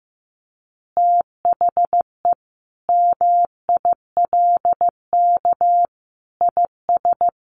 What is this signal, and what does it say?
Morse code, heard as a keyed tone, about 15 wpm, 700 Hz. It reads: THE MILK IS
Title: Additional Easy Phrases 15wpm